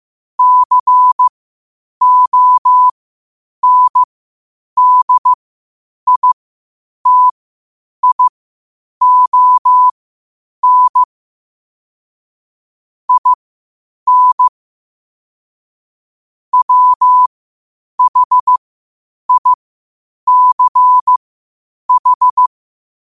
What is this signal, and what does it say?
Morse code, heard as a keyed tone, about 15 wpm, spaced as for 8 wpm, 1000 Hz. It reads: CONDITION IN WHICH
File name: qotdfiles / 8wpm